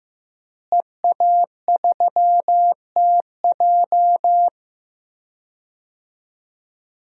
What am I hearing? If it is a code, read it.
Morse code, heard as a keyed tone, about 15 wpm, 700 Hz. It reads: EA3TJ
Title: DXCC-100 Call Signs - 5 Characters - Repeated 3x - 15wpm